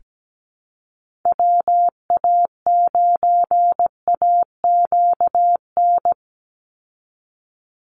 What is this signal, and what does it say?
Morse code, heard as a keyed tone, about 17 wpm, 700 Hz. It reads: WA9AQN
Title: Call Signs Repeated 2x - 17wpm